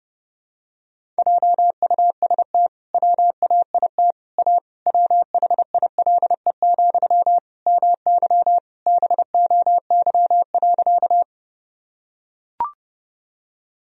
Morse code, heard as a keyed tone, about 30 wpm, 700 Hz. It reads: JUST WAIT A WHILE, MY BOY.
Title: Sentences from Top 1000 Words - Review Part 1 - 30wpm